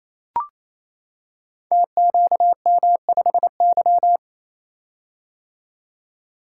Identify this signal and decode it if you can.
Morse code, heard as a keyed tone, about 28 wpm, 700 Hz. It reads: TQM5Y
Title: Code Groups - 5 Letter:Numbers - 28wpm